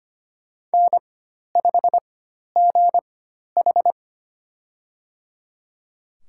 Morse code, heard as a keyed tone, about 25 wpm, spaced as for 12 wpm, 700 Hz. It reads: N5GH
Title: USA Call Signs - 4x Character Spacing - 25wpm